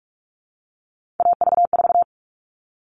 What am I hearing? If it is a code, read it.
Morse code, heard as a keyed tone, about 44 wpm, 700 Hz. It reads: AV4
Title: All Cut Numbers - 3-Digit 45wpm